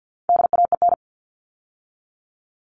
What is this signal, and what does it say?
Morse code, heard as a keyed tone, about 48 wpm, 700 Hz. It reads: DAEN
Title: All Cut Numbers - 4-Digit 50wpm